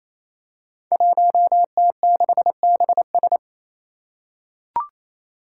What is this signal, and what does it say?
Morse code, heard as a keyed tone, about 28 wpm, 700 Hz. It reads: 1T6BS
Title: Five Challenging Letter-Number Combinations 28wpm